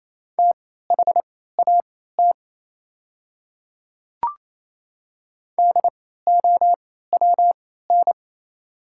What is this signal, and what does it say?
Morse code, heard as a keyed tone, about 28 wpm, spaced as for 16 wpm, 690 Hz. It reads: THAT DOWN
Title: 3x Character Spacing - Top 100 Words - 28wpm